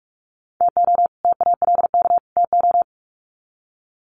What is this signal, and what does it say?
Morse code, heard as a keyed tone, about 44 wpm, 700 Hz. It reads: TO TALK TO
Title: Sets of 3 Words 45wpm